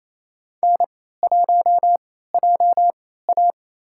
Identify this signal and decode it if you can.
Morse code, heard as a keyed tone, about 28 wpm, spaced as for 16 wpm, 700 Hz. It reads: N1JA
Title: USA Call Signs - 4 Characters - 3x Character Spacing - 28wpm